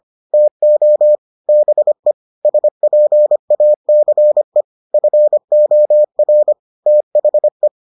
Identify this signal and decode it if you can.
Morse code, heard as a keyed tone, about 25 wpm, 600 Hz. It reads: TO BE SPACE FOR THE